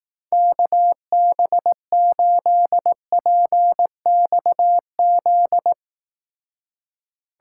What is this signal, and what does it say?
Morse code, heard as a keyed tone, about 18 wpm, 700 Hz. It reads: KB8PXZ
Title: Speed Racing - USA 2x3 Call Signs 15wpm